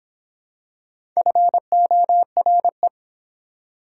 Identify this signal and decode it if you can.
Morse code, heard as a keyed tone, about 25 wpm, 700 Hz. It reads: FORE